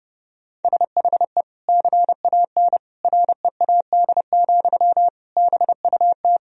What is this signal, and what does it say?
Morse code, heard as a keyed tone, about 30 wpm, 700 Hz. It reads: SHE CAN READ, BUT
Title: Sentences from Top 1000 Words - Review Part 1 - 30wpm